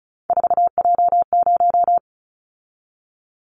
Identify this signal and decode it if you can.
Morse code, heard as a keyed tone, about 34 wpm, 700 Hz. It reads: 4J0
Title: Callsign Prefixes for Top 100 DXCC - 35wpm